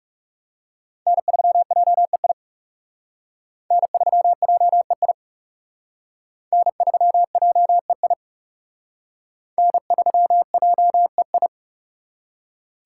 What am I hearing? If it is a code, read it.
Morse code, heard as a keyed tone, about 36 wpm, 700 Hz. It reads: N3JEI N3JEI N3JEI N3JEI
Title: Speed Racing - USA 1x3 Call Signs 30wpm